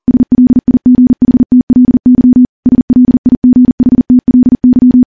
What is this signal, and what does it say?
Morse code, heard as a keyed tone, about 40 wpm, 250 Hz. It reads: SLIGHTLY SLIGHTLY
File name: morse / morse-training-40WPM